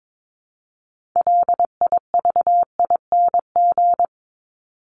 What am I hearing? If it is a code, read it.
Morse code, heard as a keyed tone, about 22 wpm, 700 Hz. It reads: LIVING
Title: ING Suffix - Top 100 Words with Base Words 22wpm